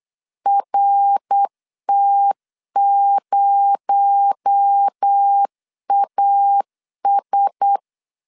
Morse code, heard as a keyed tone, about 8 wpm, 800 Hz. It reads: RT0AS
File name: LEZIONI / 6 PKX0 / 6 PKX0 8wpm 3341